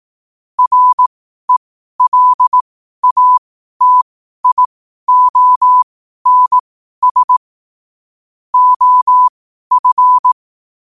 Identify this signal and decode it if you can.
Morse code, heard as a keyed tone, about 18 wpm, spaced as for 12 wpm, 1000 Hz. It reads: RELATIONS OF
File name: qotdfiles / 12wpm